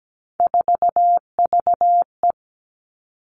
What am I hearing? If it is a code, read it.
Morse code, heard as a keyed tone, about 17 wpm, 700 Hz. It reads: 4VE
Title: Three Challenging Letter-Number Combinations 17wpm